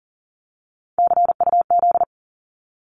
Morse code, heard as a keyed tone, about 42 wpm, 700 Hz. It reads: CUZ
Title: Abbreviations 40wpm